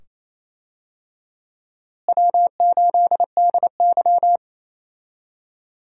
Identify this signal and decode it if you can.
Morse code, heard as a keyed tone, about 28 wpm, 700 Hz. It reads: W8DY
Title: USA Call Signs - 4 Characters - Repeated 2x - 28wpm